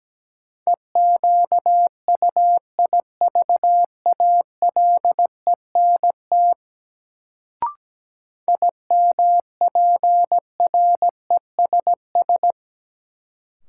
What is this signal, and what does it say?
Morse code, heard as a keyed tone, about 17 wpm, 700 Hz. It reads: EQUIVALENT IMPRESS
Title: Sentences from Top 1600 Words - New Word Drill - 17wpm